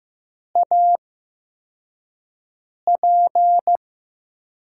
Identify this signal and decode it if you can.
Morse code, heard as a keyed tone, about 15 wpm, spaced as for 4 wpm, 700 Hz. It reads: AP
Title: USA Call Signs - 5 Characters - 8x Character Spacing - 15wpm